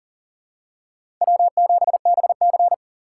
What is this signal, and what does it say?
Morse code, heard as a keyed tone, about 39 wpm, 700 Hz. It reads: W7BC
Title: USA Call Signs - 4 Characters - 2x Character Spacing - 40wpm